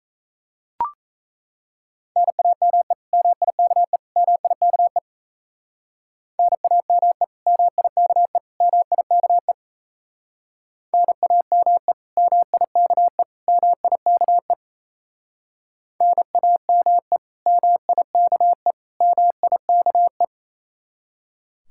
Morse code, heard as a keyed tone, about 34 wpm, 700 Hz. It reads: NAME MIKE MIKE NAME MIKE MIKE NAME MIKE MIKE NAME MIKE MIKE
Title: Speed-Racing Common QSO Phrases Encore - 28wpm